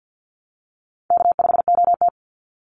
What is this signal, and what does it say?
Morse code, heard as a keyed tone, about 48 wpm, 700 Hz. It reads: K5OT
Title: NA SOTA Chaser Callsigns 50wpm